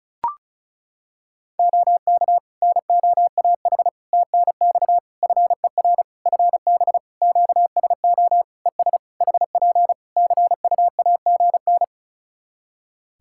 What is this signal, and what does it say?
Morse code, heard as a keyed tone, about 34 wpm, 700 Hz. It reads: OK NOAH TNX FER FB QSO ES HP CUAGN
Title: Practice QSO - Exchange 4 - Ending - 35wpm